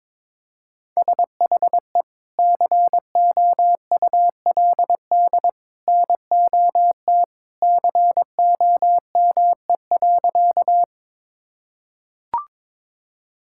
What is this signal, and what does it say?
Morse code, heard as a keyed tone, about 22 wpm, 700 Hz. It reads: SHE COULD NOT COME.
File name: practice / 22wpm / Sentences with 2-Word Phrases - 22wpm